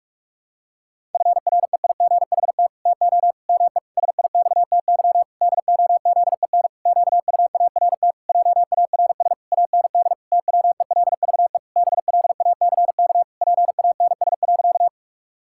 Morse code, heard as a keyed tone, about 41 wpm, 700 Hz. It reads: FREIGHT TO ME SIXTY DOZEN QUART JARS AND TWELVE BLACK PANS.
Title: Pangram Challenge v2 - 45wpm